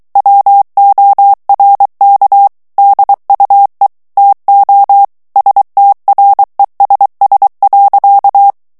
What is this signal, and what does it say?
Morse code, heard as a keyed tone, about 23 wpm, 800 Hz. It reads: WORK DUE TO STRESS.
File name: morse mp3 / ANGLAIS texte clair 1 vitesse 22 mots minute